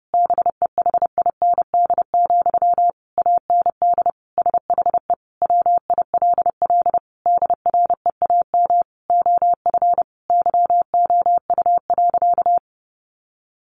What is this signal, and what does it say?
Morse code, heard as a keyed tone, about 28 wpm, 700 Hz. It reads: BEHIND, AND SHE WILL DREAM OF YOU.
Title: Sentences from Top 1500 Words 30wpm